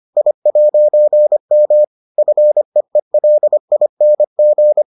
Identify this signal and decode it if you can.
Morse code, heard as a keyed tone, about 25 wpm, 600 Hz. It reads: I'M FEELING